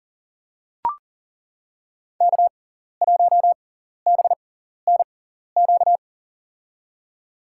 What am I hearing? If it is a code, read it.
Morse code, heard as a keyed tone, about 40 wpm, spaced as for 14 wpm, 700 Hz. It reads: K1BNQ
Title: USA Call Signs - 5 Characters - 6x Character Spacing - 40wpm